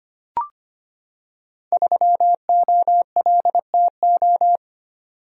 Morse code, heard as a keyed tone, about 25 wpm, 700 Hz. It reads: 3OLTO
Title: Code Groups - 5 Letter:Numbers - 25wpm